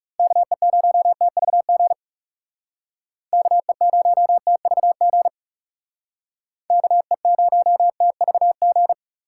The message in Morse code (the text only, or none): KE0TVG KE0TVG KE0TVG